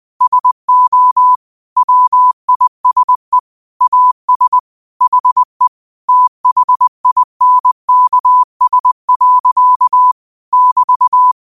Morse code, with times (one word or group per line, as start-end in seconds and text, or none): 0.2-1.4 SO
1.8-3.4 WISE
3.8-4.6 AS
5.0-5.7 HE
6.1-10.1 THINKS.
10.5-11.3 <BT>